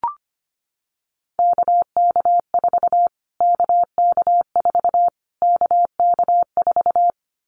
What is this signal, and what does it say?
Morse code, heard as a keyed tone, about 25 wpm, 700 Hz. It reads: KK4 KK4 KK4
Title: Callsign Prefixes for Top 25 DXCC - Repeated 3x - 25wpm